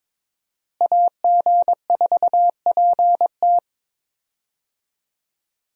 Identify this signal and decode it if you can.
Morse code, heard as a keyed tone, about 22 wpm, 700 Hz. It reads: AG4PT
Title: USA Call Signs - 5 Characters - Repeated 2x - 22wpm